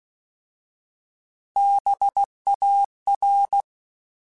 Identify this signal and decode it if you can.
Morse code, heard as a keyed tone, about 16 wpm, 800 Hz. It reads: BAR
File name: Alphabetical / Alphabetical mp3s / Bs @15